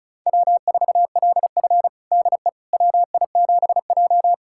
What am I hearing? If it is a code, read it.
Morse code, heard as a keyed tone, about 36 wpm, 700 Hz. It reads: W4LF DE WI7J
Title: USA Callsign DE Callsign - 4 Characters - 35wpm